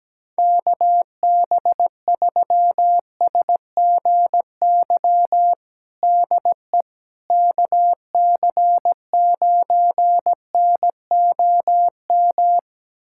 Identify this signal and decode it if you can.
Morse code, heard as a keyed tone, about 17 wpm, 700 Hz. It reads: KB3SGY DE KC9NOM
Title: USA Callsign DE Callsign - 6 Characters - 17wpm